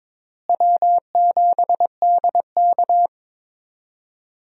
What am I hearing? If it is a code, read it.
Morse code, heard as a keyed tone, about 22 wpm, 700 Hz. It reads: W7DK
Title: USA Call Signs - 4 Characters - Repeated 2x - 22wpm